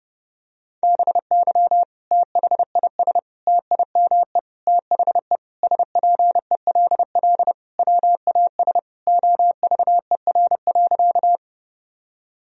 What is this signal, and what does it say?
Morse code, heard as a keyed tone, about 30 wpm, 700 Hz. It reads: BY THIS TIME THE SPELL WAS OVER.
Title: Sentences from Top 1000 Words - Review Part 1 - 30wpm